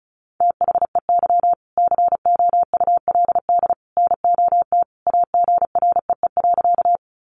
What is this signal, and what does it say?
Morse code, heard as a keyed tone, about 34 wpm, 700 Hz. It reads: THEY COULD NOT AGREE.
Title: Sentences from Top 1100 Words 35wpm